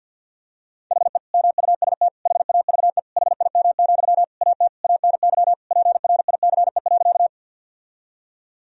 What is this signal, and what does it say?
Morse code, heard as a keyed tone, about 50 wpm, 700 Hz. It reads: HE MUST HAVE HIM, AT ANY PRICE.
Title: Sentences from Top 1000 Words - Review Part 4 - 50wpm